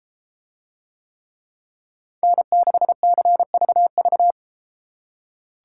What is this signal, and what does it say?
Morse code, heard as a keyed tone, about 33 wpm, 700 Hz. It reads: N6CVV